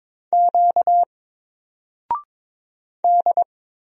Morse code, heard as a keyed tone, about 22 wpm, 700 Hz. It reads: Q D
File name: Single-Letter-Number-22wpm